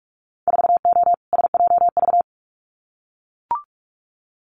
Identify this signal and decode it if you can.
Morse code, heard as a keyed tone, about 44 wpm, 700 Hz. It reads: 4O SJV
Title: Winter Field Day Exchanges Encore 45wpm